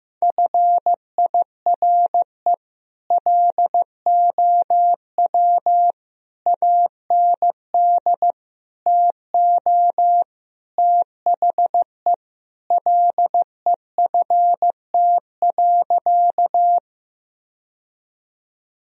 Morse code, heard as a keyed tone, about 15 wpm, 700 Hz. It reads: FIRE LOW AND TO THE LEFT.
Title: Sentences from Top 1000 Words - Review Part 1 - 15wpm